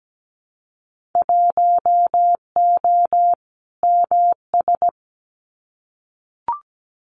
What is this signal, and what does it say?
Morse code, heard as a keyed tone, about 17 wpm, 690 Hz. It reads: 1O MS
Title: Winter Field Day Exchanges 17wpm